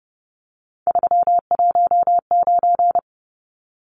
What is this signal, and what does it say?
Morse code, heard as a keyed tone, about 30 wpm, 700 Hz. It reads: 319
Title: Numbers - 3-Digit 30wpm